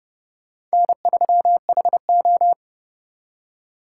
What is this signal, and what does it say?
Morse code, heard as a keyed tone, about 38 wpm, 700 Hz. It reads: N3HO